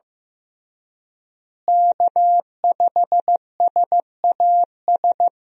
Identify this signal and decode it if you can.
Morse code, heard as a keyed tone, about 15 wpm, 700 Hz. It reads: K5SAS
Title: USA Call Signs - 5 Characters - Repeated 3x - 15wpm